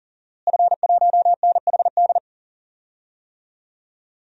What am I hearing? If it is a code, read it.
Morse code, heard as a keyed tone, about 39 wpm, 700 Hz. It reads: F1NHD
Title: DXCC-100 Call Signs - 5 Characters - Repeated 2x - 40wpm